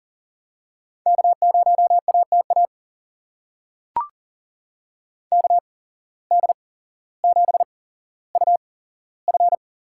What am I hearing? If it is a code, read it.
Morse code, heard as a keyed tone, about 40 wpm, spaced as for 17 wpm, 700 Hz. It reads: K0ATA KD7UF
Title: USA Call Signs - 8x Character Spacing - 40wpm